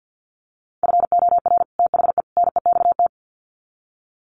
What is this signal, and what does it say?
Morse code, heard as a keyed tone, about 50 wpm, 700 Hz. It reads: FOR THE NEXT
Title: Sets of 3 Words 50wpm